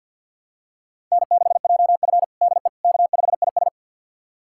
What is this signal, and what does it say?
Morse code, heard as a keyed tone, about 52 wpm, 700 Hz. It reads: N6JF DE K5IS
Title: USA Callsign DE Callsign - 4 Characters - 50wpm